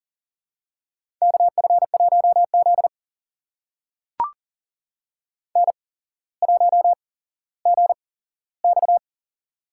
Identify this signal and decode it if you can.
Morse code, heard as a keyed tone, about 40 wpm, spaced as for 17 wpm, 700 Hz. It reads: KF1Z N1GX
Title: USA Call Signs - 4 Characters - 8x Character Spacing - 40wpm